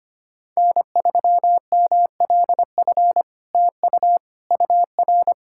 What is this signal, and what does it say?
Morse code, heard as a keyed tone, about 25 wpm, 700 Hz. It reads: N3MLF TU UR
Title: SOTA QSO Practice 25wpm